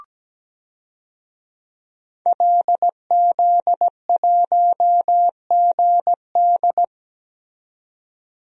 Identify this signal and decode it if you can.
Morse code, heard as a keyed tone, about 17 wpm, 700 Hz. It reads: LZ1GD
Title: DXCC-100 Call Signs - 5 Characters - 17wpm